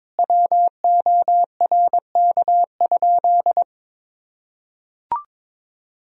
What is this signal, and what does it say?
Morse code, heard as a keyed tone, about 22 wpm, 700 Hz. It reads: WORK?
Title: Sentences from Top 1000 Words - Review Part 3 - 22wpm